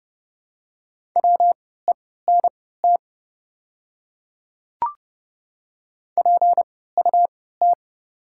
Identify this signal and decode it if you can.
Morse code, heard as a keyed tone, about 31 wpm, spaced as for 17 wpm, 690 Hz. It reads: WENT PUT
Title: 3x Character Spacing - Top 200 Words - 30wpm